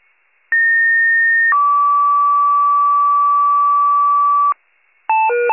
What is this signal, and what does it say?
Two-tone page, heard as a one-second tone, then a three-second tone, about 1850 then 1150 Hz, with noise. It is heard from 0.5 to 4.5 s.